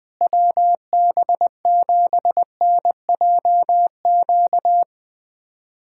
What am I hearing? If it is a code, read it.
Morse code, heard as a keyed tone, about 20 wpm, 700 Hz. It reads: WB7NJQ
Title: USA Callsign DE Callsign - 6 Characters - 20wpm